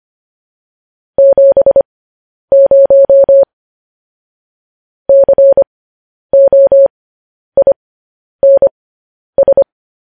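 Morse code, heard as a keyed tone, about 25 wpm, spaced as for 10 wpm, 550 Hz. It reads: 70 COINS
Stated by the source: Morse code practice of news headlines at 10 wpm.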